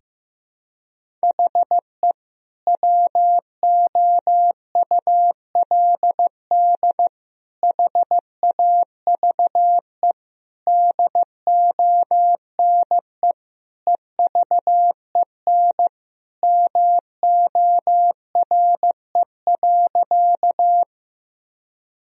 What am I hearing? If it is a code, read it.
Morse code, heard as a keyed tone, about 15 wpm, 700 Hz. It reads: HE WOULD HAVE DONE EVEN MORE.
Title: Sentences from Top 1000 Words - Review Part 1 - 15wpm